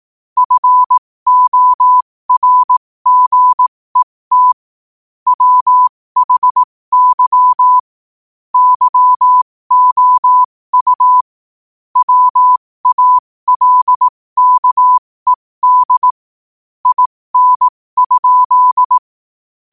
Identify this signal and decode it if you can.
Morse code, heard as a keyed tone, about 18 wpm, spaced as for 15 wpm, 1000 Hz. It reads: FORGET WHY YOU WALKED IN?
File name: qotdfiles / 15wpm